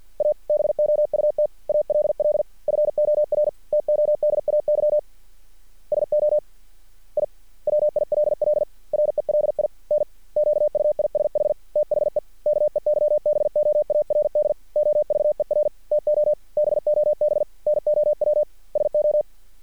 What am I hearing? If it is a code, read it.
Morse code, heard as a keyed tone, about 50 wpm, 600 Hz. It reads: ABOUT ALL FOR TODAY SO I WILL RELINQUISH THE KEYBOARD OVER TO BOB NOW SO
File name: archived / CRSnet